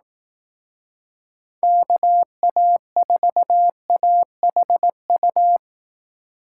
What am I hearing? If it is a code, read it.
Morse code, heard as a keyed tone, about 18 wpm, 700 Hz. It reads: KA4AHU
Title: Speed Racing - USA 2x3 Call Signs 15wpm